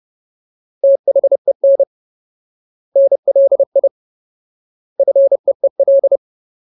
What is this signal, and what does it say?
Morse code, heard as a keyed tone, about 30 wpm, 550 Hz. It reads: THEN NLI FEEL
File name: Ultimate Word List / 30WPM Ultimate